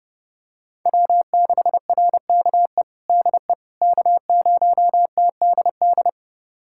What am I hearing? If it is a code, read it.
Morse code, heard as a keyed tone, about 30 wpm, 700 Hz. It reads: W6RKE DE K0TDD
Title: USA Callsign DE Callsign - 5 Characters - 30wpm